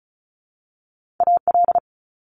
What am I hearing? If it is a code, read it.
Morse code, heard as a keyed tone, about 36 wpm, 700 Hz. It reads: AL
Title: Multi-letter Phonemes 35wpm